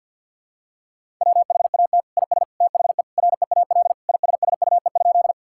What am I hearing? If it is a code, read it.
Morse code, heard as a keyed tone, about 50 wpm, 700 Hz. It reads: WHAT IS THE REAL ISSUE?
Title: Sentences from Top 1000 Words - Review Part 6 - 50wpm